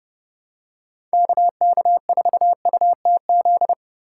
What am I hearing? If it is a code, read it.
Morse code, heard as a keyed tone, about 29 wpm, 700 Hz. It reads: KK4UTZ